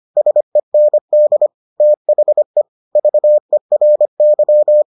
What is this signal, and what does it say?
Morse code, heard as a keyed tone, about 25 wpm, 600 Hz. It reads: SEND THE VERY